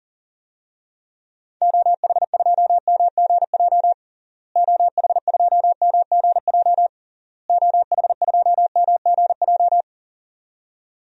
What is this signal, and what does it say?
Morse code, heard as a keyed tone, about 40 wpm, 700 Hz. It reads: OH2MGJ OH2MGJ OH2MGJ
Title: DXCC-100 Call Signs - 6 Characters - Repeated 3x - 40wpm